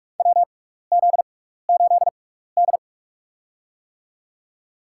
Morse code, heard as a keyed tone, about 46 wpm, spaced as for 16 wpm, 700 Hz. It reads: WZ8D
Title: USA Call Signs - 4 Characters - 6x Character Spacing - 45wpm